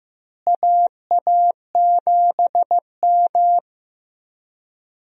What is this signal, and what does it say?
Morse code, heard as a keyed tone, about 15 wpm, 700 Hz. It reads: AA7M
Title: USA Call Signs - 4 Characters - Repeated 3x - 15wpm